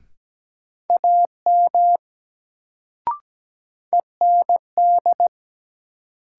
Morse code, heard as a keyed tone, about 17 wpm, 700 Hz. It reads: AM END
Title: Sets of 2 Words - New Word Drill - 17wpm